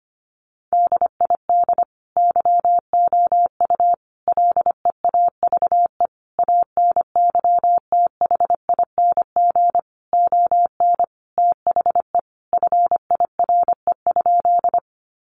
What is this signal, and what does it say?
Morse code, heard as a keyed tone, about 25 wpm, 700 Hz. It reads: DID YOU LEAVE ANYTHING ON THE FIRE?
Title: Sentences from Top 1000 Words 25wpm